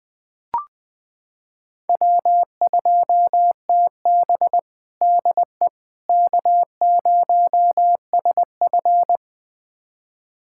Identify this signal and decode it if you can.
Morse code, heard as a keyed tone, about 20 wpm, 700 Hz. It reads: W2TB DE K0SF
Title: USA Callsign DE Callsign - 4 Characters - 20wpm